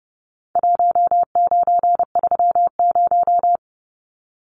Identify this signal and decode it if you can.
Morse code, heard as a keyed tone, about 30 wpm, 700 Hz. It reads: 1930
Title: Numbers - Contest Style - 4-Digit 30wpm